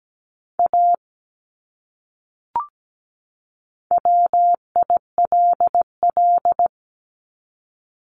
Morse code, heard as a keyed tone, about 17 wpm, 700 Hz. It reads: A WILL
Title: Top 200 Words 17wpm